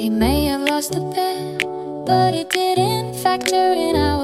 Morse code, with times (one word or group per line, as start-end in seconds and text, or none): none